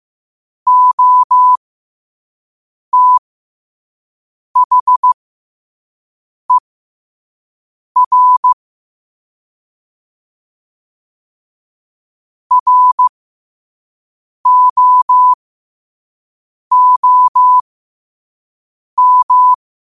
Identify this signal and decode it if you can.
Morse code, heard as a keyed tone, about 15 wpm, spaced as for 5 wpm, 1000 Hz. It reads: OTHER ROOM
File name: qotdfiles / 5wpm